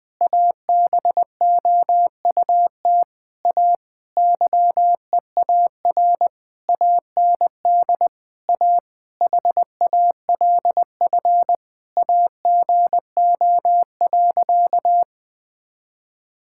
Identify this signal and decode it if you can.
Morse code, heard as a keyed tone, about 20 wpm, 700 Hz. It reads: ABOUT A YEAR AND A HALF AGO.
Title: Sentences with 7-Word Phrases - 20wpm